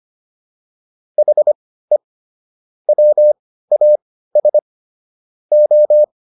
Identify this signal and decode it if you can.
Morse code, heard as a keyed tone, about 25 wpm, spaced as for 15 wpm, 600 Hz. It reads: HE WAS O
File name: WD405 15